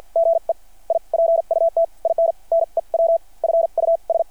Morse code, heard as a keyed tone, about 49 wpm, spaced as for 41 wpm, 660 Hz. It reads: GE I PUT IN NEW FUS
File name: archived / CRSnet